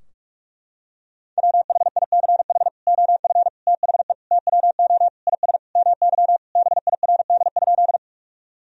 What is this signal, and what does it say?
Morse code, heard as a keyed tone, about 46 wpm, 700 Hz. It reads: WHICH OF THE TWO IS MY BIRD?
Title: Sentences from Top 1000 Words - Review Part 2 - 45wpm